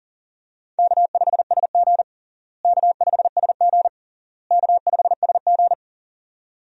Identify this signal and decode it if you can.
Morse code, heard as a keyed tone, about 40 wpm, 700 Hz. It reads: K5SG K5SG K5SG
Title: USA Call Signs - 4 Characters - Repeated 3x - 40wpm